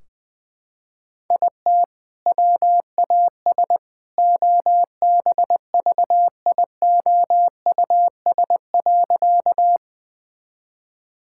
Morse code, heard as a keyed tone, about 20 wpm, 700 Hz. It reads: IT WAS OBVIOUS.
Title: Sentences from Top 1600 Words 20wpm